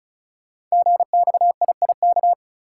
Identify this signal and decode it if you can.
Morse code, heard as a keyed tone, about 36 wpm, 690 Hz. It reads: GXIIK
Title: Code Groups - 5 Letters - 35wpm